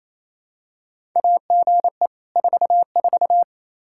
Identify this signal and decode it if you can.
Morse code, heard as a keyed tone, about 28 wpm, 700 Hz. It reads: AGE 44
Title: Practice QSO - Exchange 3 - More Info - 28wpm